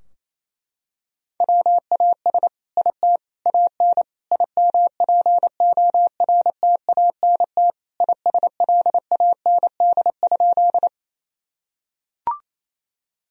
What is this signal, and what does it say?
Morse code, heard as a keyed tone, about 27 wpm, 690 Hz. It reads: WAS IT AN IMPORTANT ISLAND?
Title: Sentences from Top 1000 Words - Review Part 3 - 28wpm